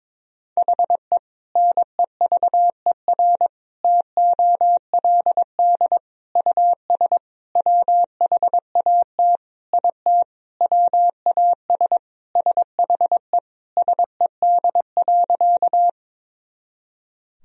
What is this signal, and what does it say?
Morse code, heard as a keyed tone, about 22 wpm, 700 Hz. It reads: HE NEVER TOLD US WHAT IT WAS SHE SED.
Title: Sentences from Top 400 Words and Abbreviations 22wpm